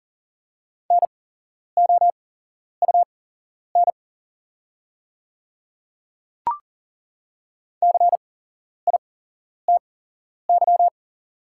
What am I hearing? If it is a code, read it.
Morse code, heard as a keyed tone, about 40 wpm, spaced as for 11 wpm, 700 Hz. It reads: NOUN CITY
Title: 8x Character Spacing - Top 500 Words - 40wpm